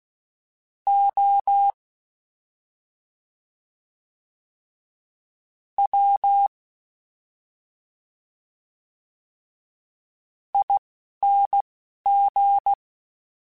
Morse code, heard as a keyed tone, about 16 wpm, spaced as for 4 wpm, 800 Hz. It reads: OWING
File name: Special INGs-mp3s / Special INGs-mp3s / Slow ing @10